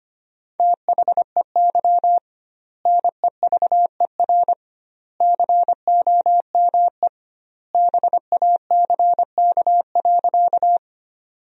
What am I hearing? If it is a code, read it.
Morse code, heard as a keyed tone, about 25 wpm, 700 Hz. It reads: THEY NEVER COME BACK.
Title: 2x Word-Spacing - Sentences from Top 300 Words - 25wpm